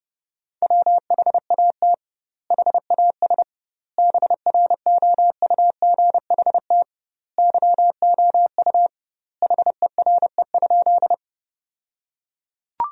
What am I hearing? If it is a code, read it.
Morse code, heard as a keyed tone, about 30 wpm, 700 Hz. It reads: WHAT HAS BROUGHT YOU HERE?
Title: 2x Word-Spacing - Sentences from Top 500 Words - 30wpm